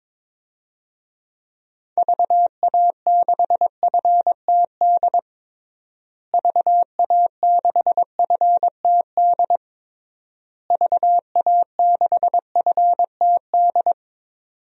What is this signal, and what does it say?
Morse code, heard as a keyed tone, about 22 wpm, 700 Hz. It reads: VA6FTD VA6FTD VA6FTD
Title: DXCC-100 Call Signs - 6 Characters - Repeated 3x - 22wpm